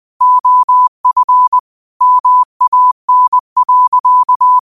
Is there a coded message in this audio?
Yes: Morse code from 0.2 to 4.6 s, heard as a keyed tone, about 20 wpm, 1000 Hz. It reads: OF MAN.